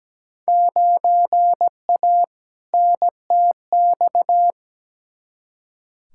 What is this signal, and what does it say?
Morse code, heard as a keyed tone, about 17 wpm, 700 Hz. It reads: 9A NTX
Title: Field Day Exchanges Encore - 17wpm